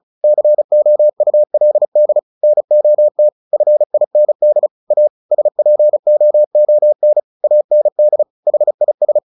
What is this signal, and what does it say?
Morse code, heard as a keyed tone, about 36 wpm, 600 Hz. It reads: COULD NOT FIND A SPOON AND HIS
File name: the old man and the sea 2 35wpm